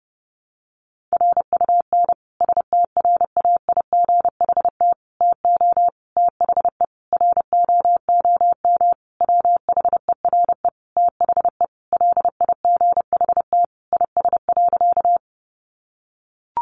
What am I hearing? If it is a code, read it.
Morse code, heard as a keyed tone, about 30 wpm, 700 Hz. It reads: RUN STRAIGHT TO THE ROOM WHERE THE LIGHT IS.
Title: Sentences from Top 1000 Words 30wpm